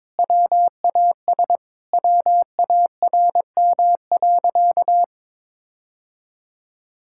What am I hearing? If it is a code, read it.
Morse code, heard as a keyed tone, about 22 wpm, 700 Hz. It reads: WAS WARM.
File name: practice / 22wpm / Sentences with 2-Word Phrases - 22wpm